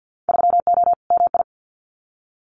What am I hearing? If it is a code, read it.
Morse code, heard as a keyed tone, about 50 wpm, 700 Hz. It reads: 3O MI
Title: Winter Field Day Exchanges 50wpm